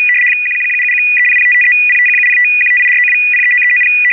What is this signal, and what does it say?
RTTY, about 75 baud. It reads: 11177130055305170158492039390
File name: RUS ITA-2-ARQ